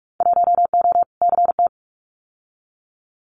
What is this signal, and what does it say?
Morse code, heard as a keyed tone, about 44 wpm, 700 Hz. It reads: 1O CT
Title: Winter Field Day Exchanges 45wpm